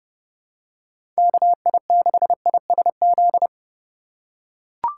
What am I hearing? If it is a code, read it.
Morse code, heard as a keyed tone, about 29 wpm, 700 Hz. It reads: KI6ISZ
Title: USA Call Signs - 6 Characters - 8x Character Spacing - 30wpm